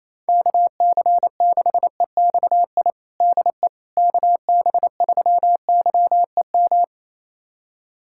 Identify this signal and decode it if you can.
Morse code, heard as a keyed tone, about 28 wpm, 700 Hz. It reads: KC6EXI DE KB3YEM
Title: USA Callsign DE Callsign 28wpm